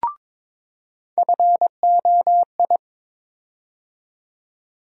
Morse code, heard as a keyed tone, about 21 wpm, 700 Hz. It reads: FOI
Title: USA Call Sign Postfixes - 3 Letters 22wpm